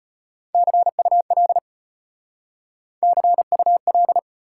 Morse code, heard as a keyed tone, about 32 wpm, 700 Hz. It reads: CUL CUL